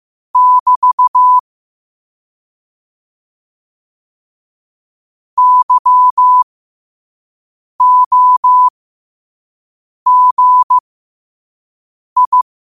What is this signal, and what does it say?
Morse code, heard as a keyed tone, about 15 wpm, spaced as for 5 wpm, 1000 Hz. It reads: <BT> YOGI
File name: qotdfiles / 5wpm